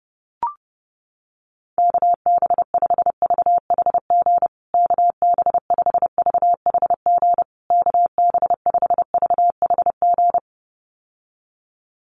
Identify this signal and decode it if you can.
Morse code, heard as a keyed tone, about 30 wpm, 700 Hz. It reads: KB5VHG KB5VHG KB5VHG
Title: USA Call Signs Repeated 3x 30wpm